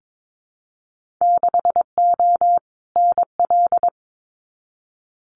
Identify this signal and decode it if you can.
Morse code, heard as a keyed tone, about 22 wpm, 700 Hz. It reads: 6O NL
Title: Winter Field Day Exchanges Encore 22wpm